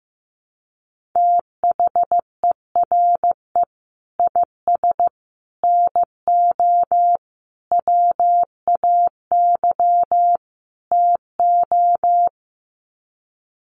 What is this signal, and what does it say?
Morse code, heard as a keyed tone, about 15 wpm, 700 Hz. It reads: THERE IS NO WAY TO
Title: Sets of 5 Words Encore 15wpm